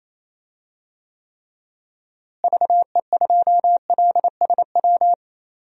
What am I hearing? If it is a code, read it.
Morse code, heard as a keyed tone, about 28 wpm, 700 Hz. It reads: VE2LSW
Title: DXCC-100 Call Signs - 6 Characters - Repeated 2x - 28wpm